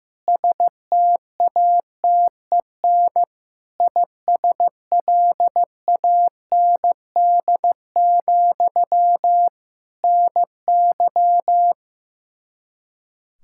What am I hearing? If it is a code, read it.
Morse code, heard as a keyed tone, about 15 wpm, 700 Hz. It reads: STATEN ISLAND, NY
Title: K1USN SST QSO Practice 15wpm